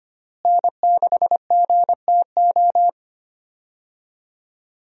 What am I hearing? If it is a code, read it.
Morse code, heard as a keyed tone, about 25 wpm, 700 Hz. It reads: N6GTO
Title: USA Call Signs - 5 Characters - Repeated 3x - 25wpm